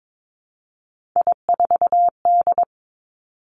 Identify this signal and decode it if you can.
Morse code, heard as a keyed tone, about 22 wpm, 700 Hz. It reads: I4D
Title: Three Challenging Letter-Number Combinations 22wpm